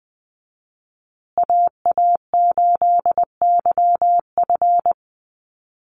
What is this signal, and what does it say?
Morse code, heard as a keyed tone, about 20 wpm, 700 Hz. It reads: AA8YF
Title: USA Call Signs Repeated 3x 20wpm